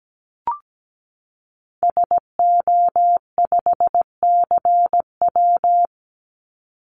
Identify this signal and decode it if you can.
Morse code, heard as a keyed tone, about 17 wpm, 700 Hz. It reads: SO5CW
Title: Call Signs Repeated 3x - 17wpm